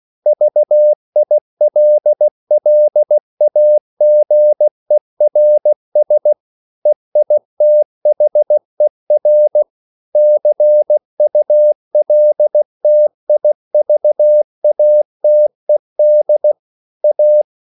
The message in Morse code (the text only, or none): VILLAGERS EITHER CULTIVATED A